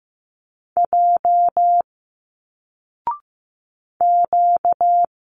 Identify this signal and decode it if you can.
Morse code, heard as a keyed tone, about 15 wpm, 700 Hz. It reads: J Q
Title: Single Letters 15wpm